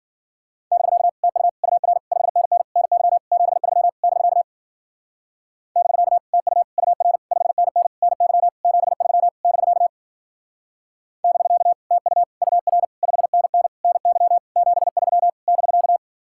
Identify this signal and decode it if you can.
Morse code, heard as a keyed tone, about 47 wpm, 700 Hz. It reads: <BK> TU UR 5NN NY 73 <BK> <BK> TU UR 5NN NY 73 <BK> <BK> TU UR 5NN NY 73 <BK>
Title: Speed-Racing Common QSO Phrases Encore - 40wpm